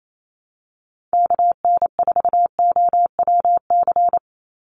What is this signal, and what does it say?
Morse code, heard as a keyed tone, about 28 wpm, 700 Hz. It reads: KN4OWC
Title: USA Call Signs 28wpm